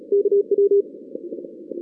Morse code, heard as a keyed tone, about 33 wpm, 400 Hz. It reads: KW
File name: 2016 arrl dx cw / 2016-ARRL-DX-CW-K5TR- - 0-17